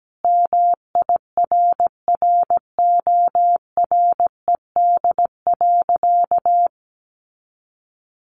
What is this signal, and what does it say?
Morse code, heard as a keyed tone, about 17 wpm, 700 Hz. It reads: MIRRORED.
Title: A Sound Of Thunder 17wpm